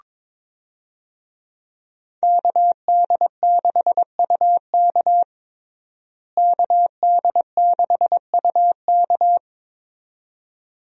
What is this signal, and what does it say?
Morse code, heard as a keyed tone, about 22 wpm, 700 Hz. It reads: KD6UK KD6UK
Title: USA Call Signs - 5 Characters - Repeated 2x - 22wpm